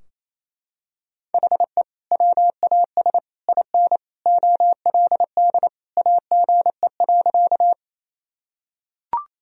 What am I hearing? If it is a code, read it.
Morse code, heard as a keyed tone, about 28 wpm, 700 Hz. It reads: HE WAS IN OLD AGE.
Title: Sentences from Top 1000 Words - Review Part 3 - 28wpm